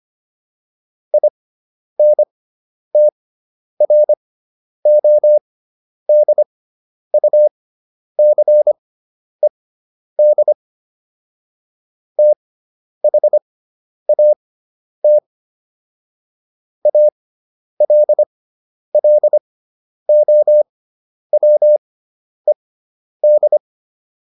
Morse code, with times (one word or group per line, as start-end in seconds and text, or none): 1.1-10.5 INTRODUCED
12.2-15.2 THAT
16.9-23.6 ALLOWED